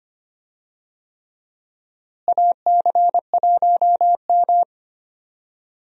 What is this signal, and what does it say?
Morse code, heard as a keyed tone, about 25 wpm, 700 Hz. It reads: AC1M
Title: USA Call Signs - 4 Characters - Repeated 3x - 25wpm